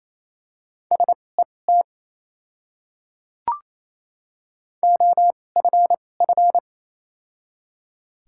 Morse code, heard as a keyed tone, about 28 wpm, spaced as for 20 wpm, 700 Hz. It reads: SET OFF
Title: 2x Character Spacing - Top 200 Words - 28wpm